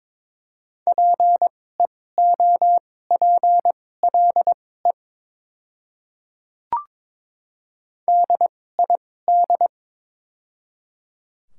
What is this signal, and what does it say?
Morse code, heard as a keyed tone, about 22 wpm, spaced as for 16 wpm, 690 Hz. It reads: PEOPLE DID
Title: 2x Character Spacing - Top 100 Words - 22wpm